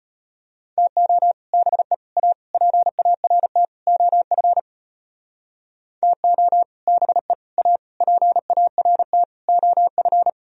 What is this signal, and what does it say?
Morse code, heard as a keyed tone, about 33 wpm, 700 Hz. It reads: TO BE A PART OF TO BE A PART OF